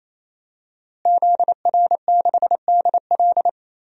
Morse code, heard as a keyed tone, about 28 wpm, 700 Hz. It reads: ZR6DL
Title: DXCC-100 Call Signs - 5 Characters - Repeated 3x - 28wpm